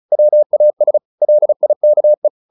Morse code, heard as a keyed tone, about 34 wpm, 600 Hz. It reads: WAS LIKE
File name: the old man and the sea / the old man and the sea 1 35wpm